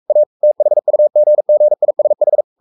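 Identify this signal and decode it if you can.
Morse code, heard as a keyed tone, about 44 wpm, 600 Hz. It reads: A THUGGISH
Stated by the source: Morse code at 43 wpm sending Ken Follett's novel Never in English, from CW med Gnister.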